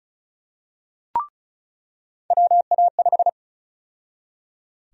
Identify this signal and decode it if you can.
Morse code, heard as a keyed tone, about 36 wpm, 700 Hz. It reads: WA5
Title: USA Call Sign Prefixes - 35wpm